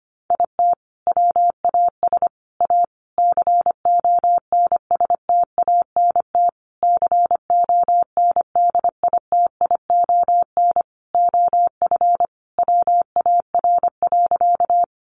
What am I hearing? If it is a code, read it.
Morse code, heard as a keyed tone, about 25 wpm, 700 Hz. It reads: IT WAS A CONSTANT CONDITION OF WAR.
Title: Sentences from Top 1500 Words 25wpm